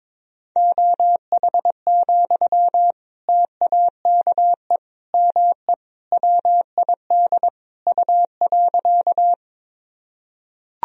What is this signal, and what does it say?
Morse code, heard as a keyed tone, about 21 wpm, 700 Hz. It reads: OH, TAKE ME WID U.
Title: Sentences from Top 500 Words and Abbreviations 22wpm